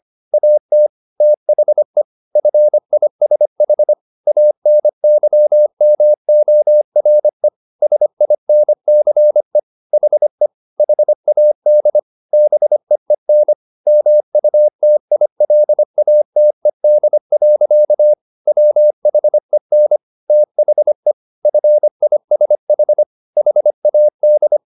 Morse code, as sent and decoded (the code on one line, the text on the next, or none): .- - / - .... / ..-. .. ... .... / .- -. -.-- -- --- .-. / ... .. -. -.-. / .... / .... .- -.. / -... . -. / -- ..- - .. .-.. .- - . -.. .-.-.- / .-- .... -. / - .... / ..-. .. ... .... / .... .- -..
AT THE FISH ANYMORE SINCE HE HAD BEEN MUTILATED. WHEN THE FISH HAD